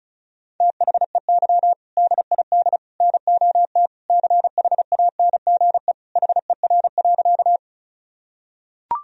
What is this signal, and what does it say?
Morse code, heard as a keyed tone, about 36 wpm, 700 Hz. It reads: THEY DID NOT CHANGE HER.
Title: Sentences from Top 1000 Words - Review Part 1 - 35wpm